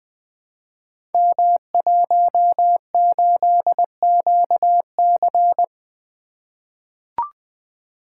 Morse code, heard as a keyed tone, about 21 wpm, 700 Hz. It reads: M18QC
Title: Code Groups - 5 Letter:Numbers - 20wpm